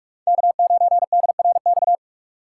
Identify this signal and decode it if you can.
Morse code, heard as a keyed tone, about 46 wpm, 700 Hz. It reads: K9DRX
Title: USA Call Signs - 5 Characters - 4x Character Spacing - 45wpm